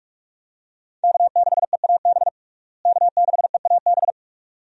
Morse code, heard as a keyed tone, about 46 wpm, 700 Hz. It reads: K6EAB K6EAB
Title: USA Call Signs - 5 Characters - Repeated 2x - 45wpm